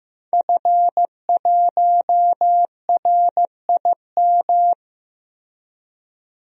Morse code, heard as a keyed tone, about 15 wpm, 700 Hz. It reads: F1RIM
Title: DXCC-100 Call Signs - 5 Characters - Repeated 2x - 15wpm